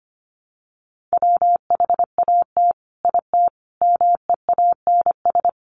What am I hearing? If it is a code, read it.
Morse code, heard as a keyed tone, about 25 wpm, 700 Hz. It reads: WHAT IT MEANS
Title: Sets of 5 Words 25wpm